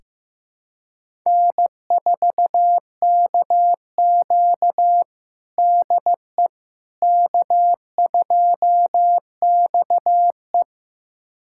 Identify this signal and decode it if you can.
Morse code, heard as a keyed tone, about 16 wpm, 700 Hz. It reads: N4KQ DE K2XE
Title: USA Callsign DE Callsign - 4 Characters - 15wpm